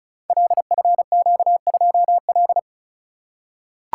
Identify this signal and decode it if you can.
Morse code, heard as a keyed tone, about 36 wpm, 700 Hz. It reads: LFQ2L
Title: Code Groups - 5 Letter:Numbers - 35wpm